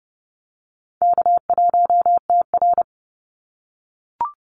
Morse code, heard as a keyed tone, about 30 wpm, 700 Hz. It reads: K1TR
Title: Easy Call Signs - 30wpm